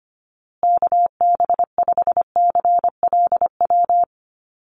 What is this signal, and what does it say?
Morse code, heard as a keyed tone, about 25 wpm, 700 Hz. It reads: KB5CLW
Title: USA Call Signs Repeated 2x 25wpm